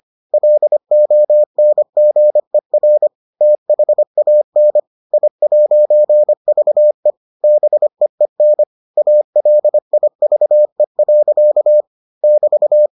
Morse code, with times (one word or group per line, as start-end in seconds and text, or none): 0.3-3.1 LONGER
3.4-4.8 THAN
5.1-7.1 I'VE
7.4-8.6 BEEN
9.0-11.8 ALIVE.
12.2-12.9 <BT>